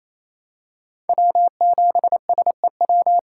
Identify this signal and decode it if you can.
Morse code, heard as a keyed tone, about 28 wpm, 700 Hz. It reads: W7SEW
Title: USA Call Signs - 6x Character Spacing - 28wpm